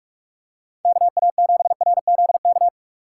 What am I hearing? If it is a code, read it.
Morse code, heard as a keyed tone, about 44 wpm, 700 Hz. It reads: KA7RZK